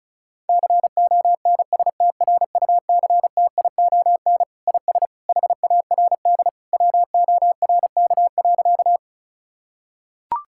Morse code, heard as a keyed tone, about 34 wpm, 700 Hz. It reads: CONSTRUCTION IS HARD WORK.
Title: Sentences from Top 1600 Words 35wpm